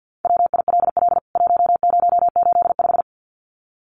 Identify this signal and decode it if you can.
Morse code, heard as a keyed tone, about 48 wpm, 700 Hz. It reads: WILL 1085
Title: CWT Exchange 50wpm